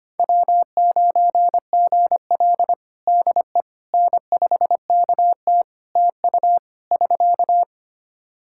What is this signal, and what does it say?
Morse code, heard as a keyed tone, about 25 wpm, 700 Hz. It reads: W9GL DE N5KT TU <SK>
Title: Practice QSO - Exchange 4 - Ending - 25wpm